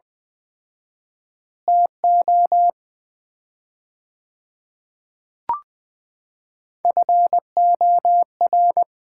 Morse code, heard as a keyed tone, about 20 wpm, 700 Hz. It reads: TO FOR
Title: 4x Word-Spacing - Sets of 2 Words - 20wpm